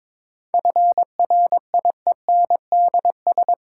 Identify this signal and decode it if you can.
Morse code, heard as a keyed tone, about 21 wpm, 700 Hz. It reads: FRIENDS
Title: Sentences from Top 1600 Words - New Word Drill Alt - 22wpm